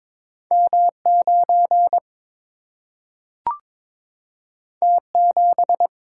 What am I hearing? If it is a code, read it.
Morse code, heard as a keyed tone, about 22 wpm, 700 Hz. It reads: M 9 T 7
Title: Single-Letter Single-Number 22wpm